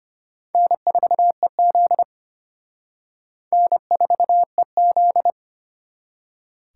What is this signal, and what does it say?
Morse code, heard as a keyed tone, about 31 wpm, 700 Hz. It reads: N4EZ N4EZ